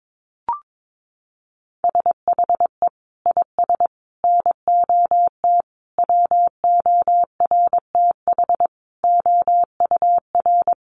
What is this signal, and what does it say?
Morse code, heard as a keyed tone, about 22 wpm, 700 Hz. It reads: SHE IS NOT WORTH OUR
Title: Sentences from Top 1100 Words 22wpm